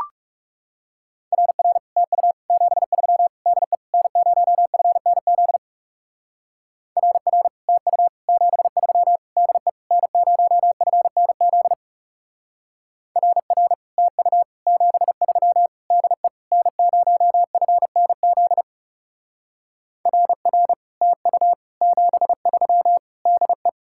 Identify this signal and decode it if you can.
Morse code, heard as a keyed tone, about 39 wpm, 700 Hz. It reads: RR TU 73 DE N0FNZ RR TU 73 DE N0FNZ RR TU 73 DE N0FNZ RR TU 73 DE